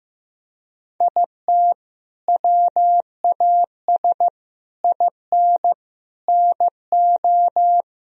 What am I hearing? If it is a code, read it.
Morse code, heard as a keyed tone, about 15 wpm, 700 Hz. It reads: IT WAS IN NO
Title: Sentences from Top 1000 Words - Review Part 4 - 15wpm